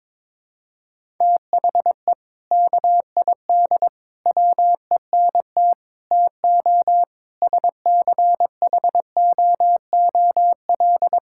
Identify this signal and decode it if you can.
Morse code, heard as a keyed tone, about 22 wpm, 700 Hz. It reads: THE KID WENT TO SCHOOL
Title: Sentences from Top 1600 Words 22wpm